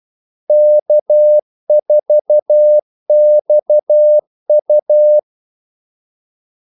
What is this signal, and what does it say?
Morse code, heard as a keyed tone, about 12 wpm, 600 Hz. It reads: K4XU
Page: 2 ekstra ordmellemrum mellem kaldesignalerne